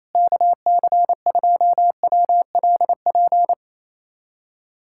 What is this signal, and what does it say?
Morse code, heard as a keyed tone, about 28 wpm, 700 Hz. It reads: KC2WLP
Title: USA Callsign DE Callsign 28wpm